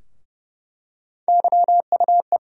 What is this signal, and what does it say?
Morse code, heard as a keyed tone, about 30 wpm, 700 Hz. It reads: YUE
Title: USA Call Sign Postfixes - 3 Letters 30wpm